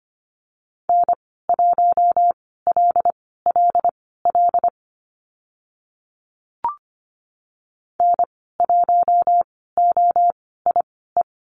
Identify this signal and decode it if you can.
Morse code, heard as a keyed tone, about 25 wpm, spaced as for 16 wpm, 700 Hz. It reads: N1LLL N1OIE
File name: Lesson 15g - Number 5 - Call Signs 25wpm